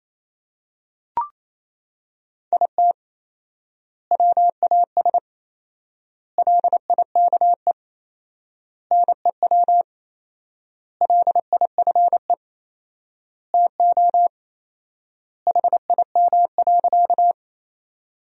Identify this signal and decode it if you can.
Morse code, heard as a keyed tone, about 27 wpm, 700 Hz. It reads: IT WAS LIKE NEW LIFE TO HIM.
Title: 4x Word-Spacing - Sentences from Top 300 Words - 28wpm